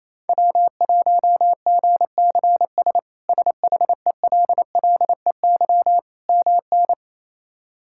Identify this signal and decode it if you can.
Morse code, heard as a keyed tone, about 28 wpm, 700 Hz. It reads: W1GCS SHELLEY MN
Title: K1USN SST QSO Practice - 28wpm